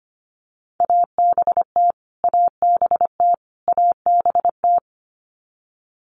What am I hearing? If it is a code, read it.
Morse code, heard as a keyed tone, about 25 wpm, 700 Hz. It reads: ABT ABT ABT
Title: Abbreviations Repeated 3x - 25wpm